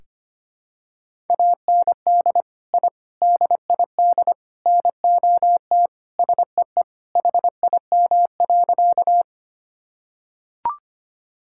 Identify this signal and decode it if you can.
Morse code, heard as a keyed tone, about 25 wpm, 700 Hz. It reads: AND I DID NOT SEE HIM.
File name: practice / 25wpm / Sentences with 4-Word Phrases - 25wpm